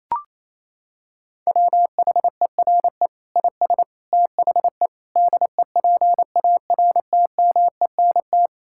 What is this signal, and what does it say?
Morse code, heard as a keyed tone, about 28 wpm, 700 Hz. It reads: WHERE IS THE DEPARTMENT
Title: Sentences from Top 1000 Words - Review Part 5 - 28wpm